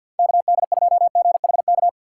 Morse code, heard as a keyed tone, about 52 wpm, 700 Hz. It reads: KD2GHK
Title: USA Call Signs - 6 Characters - 3x Character Spacing - 50wpm